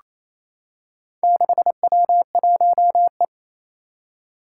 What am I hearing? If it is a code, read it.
Morse code, heard as a keyed tone, about 27 wpm, 700 Hz. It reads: 6W1E
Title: Four Challenging Letter-Number Combinations 28wpm